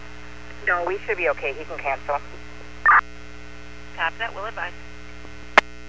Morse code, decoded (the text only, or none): none